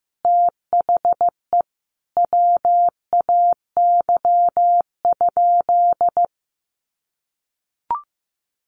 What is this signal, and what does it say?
Morse code, heard as a keyed tone, about 15 wpm, 700 Hz. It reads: THE WAY?
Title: Sentences from Top 1100 Words 15wpm